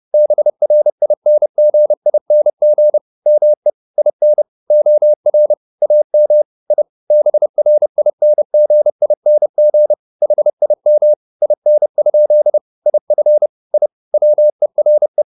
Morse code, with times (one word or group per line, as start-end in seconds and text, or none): 0.1-3.0 BRINGING
3.3-3.7 ME
4.0-4.4 IN
4.7-5.5 OR
5.8-6.4 AM
6.7-6.8 I
7.1-9.9 BRINGING
10.2-11.1 HIM
11.4-12.6 IN?
12.9-13.5 IF
13.7-13.9 I
14.1-15.2 WERE